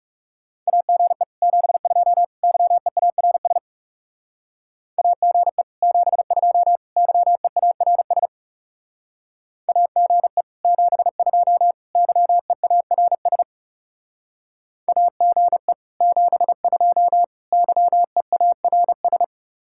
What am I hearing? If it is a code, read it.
Morse code, heard as a keyed tone, about 35 wpm, 700 Hz. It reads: AGE 72 YEARS AGE 72 YEARS AGE 72 YEARS AGE 72 YEARS
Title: Speed-Racing Common QSO Phrases - 30wpm